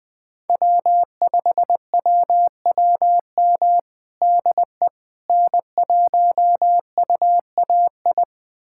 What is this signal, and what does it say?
Morse code, heard as a keyed tone, about 20 wpm, 700 Hz. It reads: W5WWM DE N1UAI
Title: USA Callsign DE Callsign - 5 Characters - 20wpm